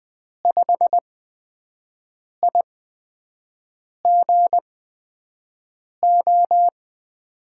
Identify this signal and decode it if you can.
Morse code, heard as a keyed tone, about 20 wpm, spaced as for 5 wpm, 700 Hz. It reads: 5IGO
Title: USA Call Signs - 8x Character Spacing - 20wpm